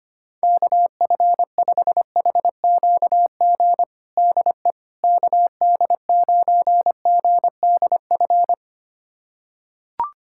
Morse code, heard as a keyed tone, about 25 wpm, 700 Hz. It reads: KF5HQG DE KD9GDF
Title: USA Callsign DE Callsign 25wpm